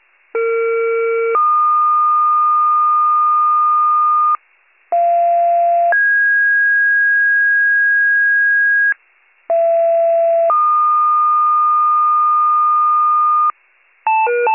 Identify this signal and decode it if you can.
Two-tone page, heard as a one-second tone, then a three-second tone, about 460 then 1220 Hz, about 690 then 1650 Hz, about 670 then 1220 Hz. It is heard from 0.3 to 13.5 s.